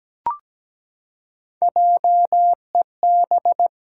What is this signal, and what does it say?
Morse code, heard as a keyed tone, about 17 wpm, 700 Hz. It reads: JEB
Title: Pangram Challenge v2 - 17wpm